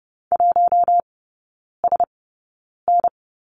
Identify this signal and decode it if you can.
Morse code, heard as a keyed tone, about 30 wpm, spaced as for 9 wpm, 700 Hz. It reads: 1SN
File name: Lesson 16e - Letter c - Three Characters 30wpm